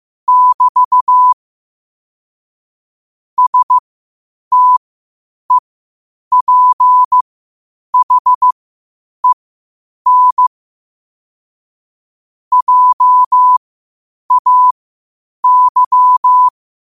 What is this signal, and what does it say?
Morse code, heard as a keyed tone, about 15 wpm, spaced as for 8 wpm, 1000 Hz. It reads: <BT> STEPHEN JAY
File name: qotdfiles / 8wpm